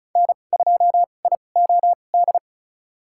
Morse code, heard as a keyed tone, about 36 wpm, spaced as for 25 wpm, 700 Hz. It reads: N2IOD
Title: USA Call Signs - 5 Characters - 2x Character Spacing - 35wpm